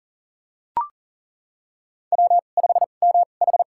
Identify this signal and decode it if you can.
Morse code, heard as a keyed tone, about 40 wpm, spaced as for 29 wpm, 700 Hz. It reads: W5MH
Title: USA Call Signs - 4 Characters - 2x Character Spacing - 40wpm